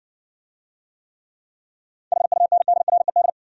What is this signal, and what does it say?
Morse code, heard as a keyed tone, about 61 wpm, 700 Hz. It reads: HUNDRED